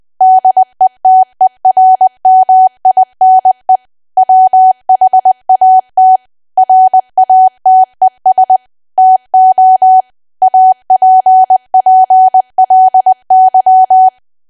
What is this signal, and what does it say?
Morse code, heard as a keyed tone, about 20 wpm, 750 Hz. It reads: DETERMINE WHAT RATES TO APPLY
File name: cwa / W1AW-Level3 / 20WPM